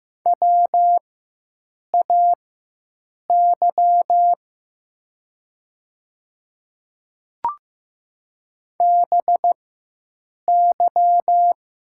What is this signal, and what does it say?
Morse code, heard as a keyed tone, about 15 wpm, spaced as for 7 wpm, 700 Hz. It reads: WAY BY
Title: 4x Character Spacing - Top 100 Words - 15wpm